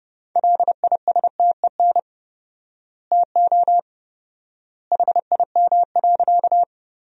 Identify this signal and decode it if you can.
Morse code, heard as a keyed tone, about 30 wpm, 700 Hz. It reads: LISTEN TO HIM.
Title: 4x Word-Spacing - Sentences from Top 400 Words - 30wpm